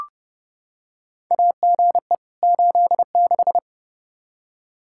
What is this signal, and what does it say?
Morse code, heard as a keyed tone, about 30 wpm, 700 Hz. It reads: AGE 86
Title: K1USN SST QSO Practice 30wpm